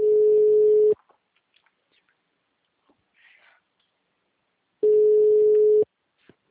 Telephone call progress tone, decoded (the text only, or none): ringback tone